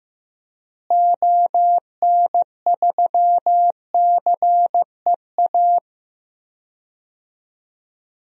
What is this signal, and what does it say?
Morse code, heard as a keyed tone, about 15 wpm, 700 Hz. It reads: ON3CEA
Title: DXCC-100 Call Signs - 6 Characters - Repeated 3x - 15wpm